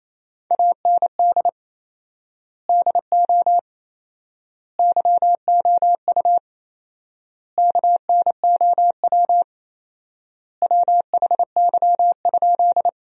Morse code, heard as a keyed tone, about 28 wpm, 700 Hz. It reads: AND DO YOU KNOW WHY?
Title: 4x Word-Spacing - Sentences from Top 200 Words - 28wpm